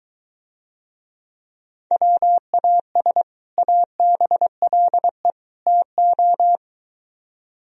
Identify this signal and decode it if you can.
Morse code, heard as a keyed tone, about 23 wpm, 700 Hz. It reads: WAS ABLE TO